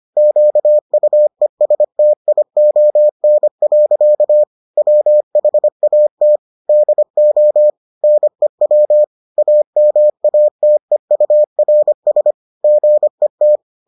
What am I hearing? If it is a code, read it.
Morse code, heard as a keyed tone, about 25 wpm, 600 Hz. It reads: QUESTION. WHAT DO NEW AMATEURS GET